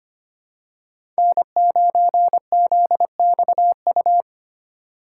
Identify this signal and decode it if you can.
Morse code, heard as a keyed tone, about 25 wpm, 700 Hz. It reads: N9ZXU
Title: USA Call Signs - 3x Character Spacing - 25wpm